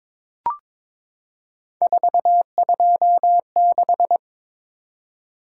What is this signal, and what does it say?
Morse code, heard as a keyed tone, about 22 wpm, 700 Hz. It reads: 426
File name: practice / 22wpm / Three Letter-Number Combinations - 22wpm